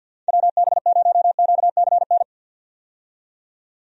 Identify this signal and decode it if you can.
Morse code, heard as a keyed tone, about 51 wpm, 700 Hz. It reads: WB0QCN
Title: USA Call Signs - 6 Characters - 50wpm